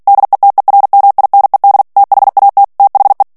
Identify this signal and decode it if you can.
Morse code, heard as a keyed tone, about 48 wpm, 800 Hz. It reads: DETERMINED THAT THE
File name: ANGLAIS texte clair 3 vitesse 45 mots minute